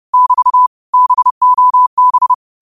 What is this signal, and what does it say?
Morse code, heard as a keyed tone, about 30 wpm, 1000 Hz. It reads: <BT> BOB
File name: qotdfiles / 30wpm